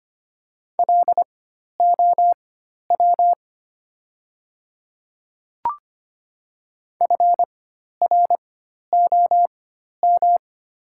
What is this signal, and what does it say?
Morse code, heard as a keyed tone, about 25 wpm, spaced as for 12 wpm, 700 Hz. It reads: LOW FROM
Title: 4x Character Spacing - Top 300 Words - 25wpm